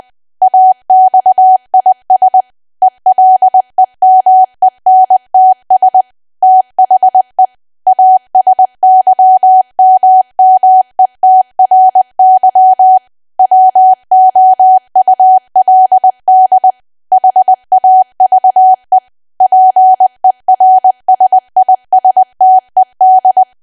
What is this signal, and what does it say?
Morse code, heard as a keyed tone, about 20 wpm, 750 Hz. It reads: AXIS ELEMENTS THE ASYMMETRY WOULD HAVE PERSISTED
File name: cwa / W1AW-Level3 / 20WPM